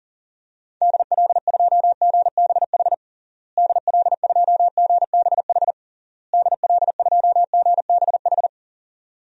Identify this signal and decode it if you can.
Morse code, heard as a keyed tone, about 37 wpm, 700 Hz. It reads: DL2GBH DL2GBH DL2GBH
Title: DXCC-100 Call Signs - 6 Characters - Repeated 3x - 40wpm